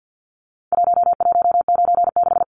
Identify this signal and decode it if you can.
Morse code, heard as a keyed tone, about 49 wpm, 700 Hz. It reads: 1196
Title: Numbers - Contest Style - 4-Digit 50wpm